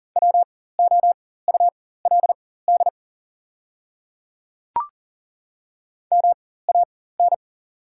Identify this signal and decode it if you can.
Morse code, heard as a keyed tone, about 40 wpm, spaced as for 19 wpm, 700 Hz. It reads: WOULD MAN
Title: 4x Character Spacing - Top 300 Words - 40wpm